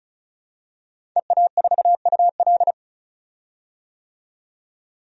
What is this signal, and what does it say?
Morse code, heard as a keyed tone, about 34 wpm, 700 Hz. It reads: EA4UL
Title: DXCC-100 Call Signs - 5 Characters - 35wpm